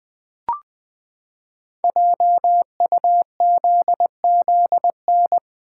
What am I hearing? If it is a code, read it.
Morse code, heard as a keyed tone, about 20 wpm, 700 Hz. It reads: JUZZN
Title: Code Groups - 5 Letters - 20wpm